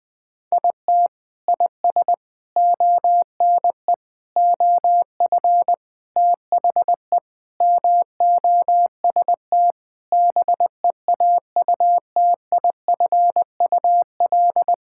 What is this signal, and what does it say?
Morse code, heard as a keyed tone, about 20 wpm, 700 Hz. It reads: IT IS ONE OF THE MOST BEAUTIFUL
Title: Sentences with 5-Word Phrases - Spoken Beginning - 20wpm